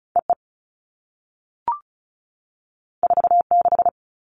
Morse code, heard as a keyed tone, about 36 wpm, 690 Hz. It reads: EE 46
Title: All Cut Numbers - 2-Digit 35wpm